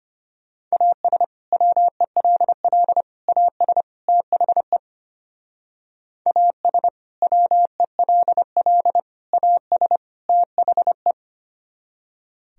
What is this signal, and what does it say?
Morse code, heard as a keyed tone, about 29 wpm, 700 Hz. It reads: AS WELL AS THE AS WELL AS THE
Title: Speed-Racing - Sets of 4 Words 25wpm